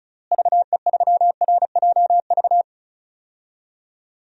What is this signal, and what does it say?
Morse code, heard as a keyed tone, about 36 wpm, 700 Hz. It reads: VE3RJV
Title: DXCC-100 Call Signs - 6 Characters - Repeated 2x - 35wpm